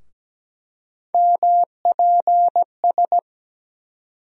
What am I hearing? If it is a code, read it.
Morse code, heard as a keyed tone, about 17 wpm, 700 Hz. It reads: MPS
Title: USA Call Sign Postfixes - 3 Letters 17wpm